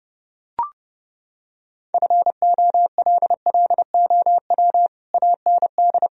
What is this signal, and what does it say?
Morse code, heard as a keyed tone, about 30 wpm, 700 Hz. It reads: FOLLOW AND
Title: Sentences from Top 1000 Words - Review Part 1 - 30wpm